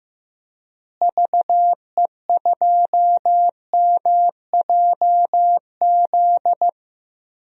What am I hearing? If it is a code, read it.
Morse code, heard as a keyed tone, about 15 wpm, 700 Hz. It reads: VE2MJZ
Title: DXCC-100 Call Signs - 6 Characters - Repeated 2x - 15wpm